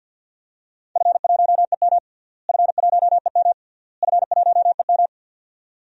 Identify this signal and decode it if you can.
Morse code, heard as a keyed tone, about 52 wpm, 700 Hz. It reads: F1EM F1EM F1EM
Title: DXCC-100 Call Signs - 4 Characters - Repeated 3x - 50wpm